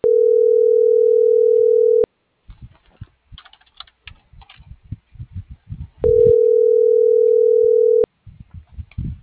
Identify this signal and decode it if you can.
Ringback tone (North America), 440 + 480 Hz.